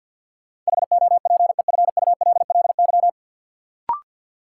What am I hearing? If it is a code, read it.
Morse code, heard as a keyed tone, about 48 wpm, 700 Hz. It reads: HOPEFULLY
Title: Sentences from Top 1500 Words - New Word Drill - 50wpm